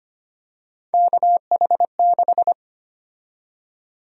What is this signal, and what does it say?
Morse code, heard as a keyed tone, about 25 wpm, 700 Hz. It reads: KH6
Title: USA Call Sign Prefixes - 25wpm